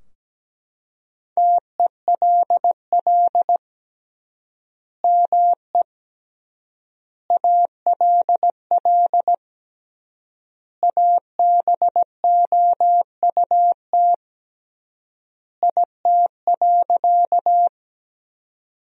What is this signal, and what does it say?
Morse code, heard as a keyed tone, about 17 wpm, 700 Hz. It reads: TELL ME ALL ABOUT IT.
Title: 3x Word-Spacing - Sentences from Top 200 Words - 17wpm